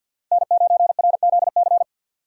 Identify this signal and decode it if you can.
Morse code, heard as a keyed tone, about 53 wpm, 700 Hz. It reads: N9RZC
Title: USA Call Signs - 5 Characters - 3x Character Spacing - 50wpm